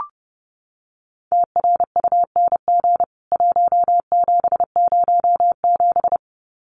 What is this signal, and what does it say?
Morse code, heard as a keyed tone, about 30 wpm, 700 Hz. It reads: TRUNG 1707
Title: CWT Exchange 30wpm